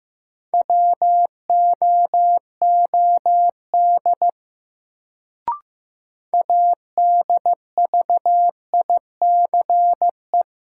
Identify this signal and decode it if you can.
Morse code, heard as a keyed tone, about 15 wpm, 700 Hz. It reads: WOOD ADVICE
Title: Sentences from Top 1100 Words - New Word Drill Alt - 15wpm